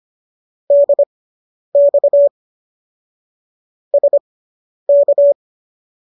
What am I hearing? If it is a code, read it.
Morse code, heard as a keyed tone, about 25 wpm, spaced as for 10 wpm, 570 Hz. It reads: DX SK